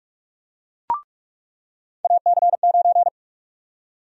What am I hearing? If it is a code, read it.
Morse code, heard as a keyed tone, about 44 wpm, 700 Hz. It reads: AC9
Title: USA Call Sign Prefixes - 45wpm